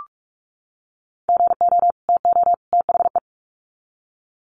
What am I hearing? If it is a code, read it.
Morse code, heard as a keyed tone, about 44 wpm, 700 Hz. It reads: GO TO THE
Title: Sets of 3 Words 45wpm